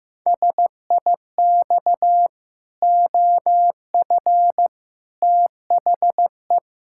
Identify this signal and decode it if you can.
Morse code, heard as a keyed tone, about 16 wpm, 700 Hz. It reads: SIX OF THE
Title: Pangram Challenge v2 - 15wpm